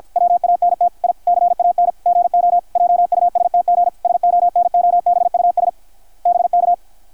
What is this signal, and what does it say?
Morse code, heard as a keyed tone, about 53 wpm, 660 Hz, with radio noise. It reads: WANT I CAN GO JUSTG SONOBUS BK
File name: archived / CRSnet